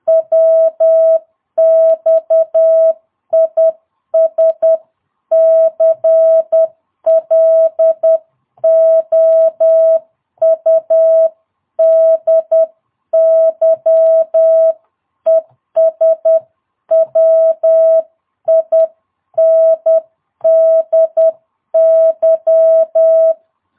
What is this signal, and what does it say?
Morse code, heard as a keyed tone, about 10 wpm, 650 Hz. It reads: WX IS CLOUDY ES WINDY
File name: jh8cbh / 2022cw